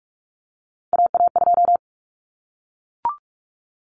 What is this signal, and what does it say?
Morse code, heard as a keyed tone, about 44 wpm, 700 Hz. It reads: AA2
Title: Common Cut Numbers - 3-Digit 45wpm